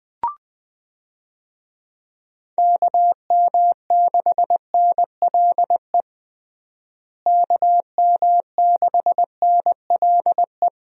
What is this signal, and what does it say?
Morse code, heard as a keyed tone, about 20 wpm, 700 Hz. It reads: KM6NLE KM6NLE
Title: USA Call Signs - 6 Characters - Repeated 3x - 20wpm